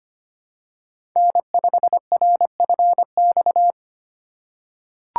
Morse code, heard as a keyed tone, about 25 wpm, 690 Hz. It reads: N5RFX
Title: USA Call Signs - 4x Character Spacing - 25wpm